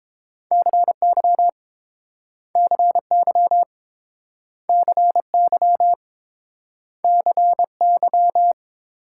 Morse code, heard as a keyed tone, about 31 wpm, 700 Hz. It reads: CY CY CY CY